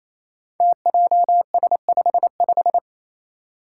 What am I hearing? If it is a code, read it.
Morse code, heard as a keyed tone, about 27 wpm, 700 Hz. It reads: TJS55
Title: Five Challenging Letter-Number Combinations 28wpm